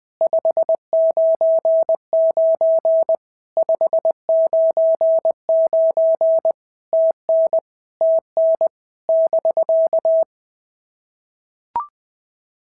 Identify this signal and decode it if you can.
Morse code, heard as a keyed tone, about 20 wpm, 650 Hz. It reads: 599 599 TN TN <BK>
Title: POTA QSO Practice - 20wpm